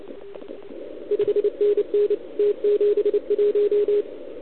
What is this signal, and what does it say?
Morse code, heard as a keyed tone, about 28 wpm, 410 Hz, with radio noise. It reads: 5NN T71
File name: ARCHIV / OKDX2019 / OKOM40 KX3